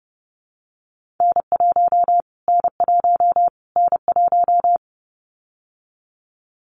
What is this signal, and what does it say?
Morse code, heard as a keyed tone, about 30 wpm, 700 Hz. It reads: N1 N1 N1
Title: Callsign Prefixes for Top 25 DXCC - Repeated 3x - 30wpm